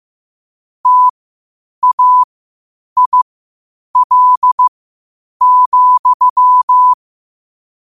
Morse code, heard as a keyed tone, about 15 wpm, spaced as for 8 wpm, 1000 Hz. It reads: TAIL,
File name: qotdfiles / 8wpm